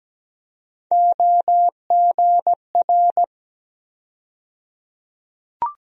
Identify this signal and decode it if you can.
Morse code, heard as a keyed tone, about 17 wpm, 700 Hz. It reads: OGR
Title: USA Call Sign Postfixes - 3 Letters 17wpm